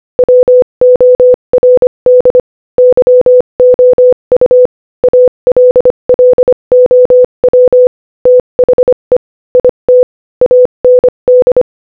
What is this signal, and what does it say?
Morse code, heard as a keyed tone, about 25 wpm, 500 Hz. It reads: WORD YOU ALLOW THE IT AND
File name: Morse-Code-Audio-Speed-25-4